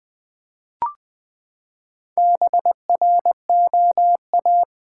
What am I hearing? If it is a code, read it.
Morse code, heard as a keyed tone, about 20 wpm, 700 Hz. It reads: BROA
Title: Build A Word - 4 Letters At A Time - 11-Letter Words - 20wpm